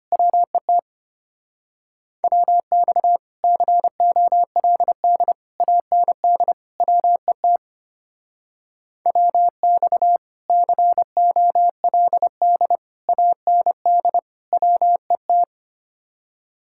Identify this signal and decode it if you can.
Morse code, heard as a keyed tone, about 31 wpm, 700 Hz. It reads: WET WX COLD AND WET WX COLD AND WET
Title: Speed-Racing Common QSO Phrases - 25wpm